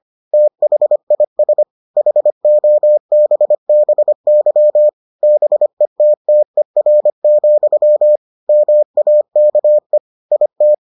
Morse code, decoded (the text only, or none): THIS HOBBY BETTER, MAKE IT